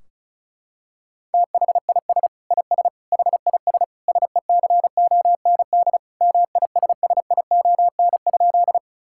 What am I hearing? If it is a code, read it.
Morse code, heard as a keyed tone, about 36 wpm, 700 Hz. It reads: THIS IS HIS SECOND MISSION?
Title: Sentences from Top 1600 Words 35wpm